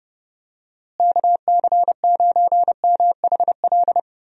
Morse code, heard as a keyed tone, about 30 wpm, 700 Hz. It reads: KC9MHL
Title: USA Call Signs - 6 Characters - Repeated 2x - 30wpm